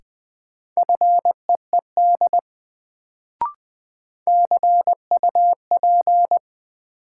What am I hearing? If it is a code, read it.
Morse code, heard as a keyed tone, about 20 wpm, 700 Hz. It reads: FEED CUP
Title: Sentences from Top 1600 Words - New Word Drill Alt - 20wpm